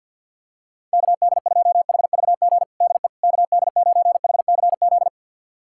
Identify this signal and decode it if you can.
Morse code, heard as a keyed tone, about 52 wpm, 700 Hz. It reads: KD2HVG DE KD9HCZ
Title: USA Callsign DE Callsign - 6 Characters - 50wpm